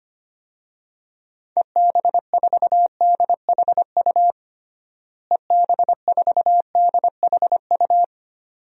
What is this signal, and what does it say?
Morse code, heard as a keyed tone, about 25 wpm, 700 Hz. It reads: EB4DHU EB4DHU
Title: DXCC-100 Call Signs - 6 Characters - Repeated 2x - 25wpm